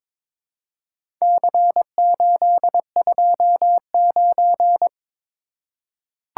Morse code, heard as a keyed tone, about 22 wpm, 700 Hz. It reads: C829
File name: practice / 22wpm / Four Letter-Number Combinations - 22wpm